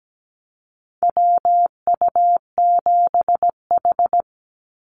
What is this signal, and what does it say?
Morse code, heard as a keyed tone, about 17 wpm, 700 Hz. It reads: WU7H
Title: NA SOTA Activator Callsigns 17wpm